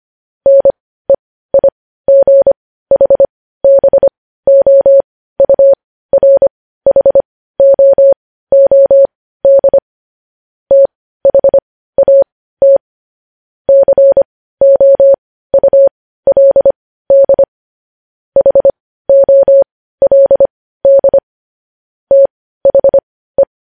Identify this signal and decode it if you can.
Morse code, heard as a keyed tone, about 25 wpm, spaced as for 15 wpm, 550 Hz. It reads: NEIGHBOURHOOD THAT COULD HOLD THE
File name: podcast / News headlines-1x-25x15